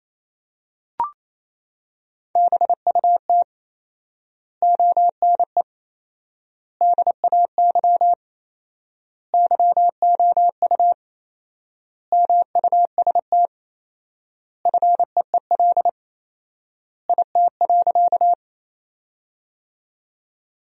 Morse code, heard as a keyed tone, about 28 wpm, 700 Hz. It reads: BUT ONE DAY YOU MUST FEEL IT.
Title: 4x Word-Spacing - Sentences from Top 400 Words - 28wpm